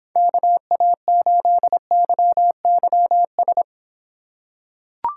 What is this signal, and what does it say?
Morse code, heard as a keyed tone, about 24 wpm, 700 Hz. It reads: KA8YYS